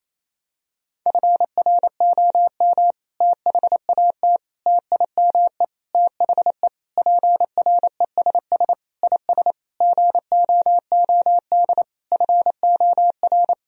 Morse code, heard as a keyed tone, about 28 wpm, 700 Hz. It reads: FROM THAT TIME THE PRESS IS GOOD FOR
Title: Sentences from Top 1000 Words - Review Part 1 - 28wpm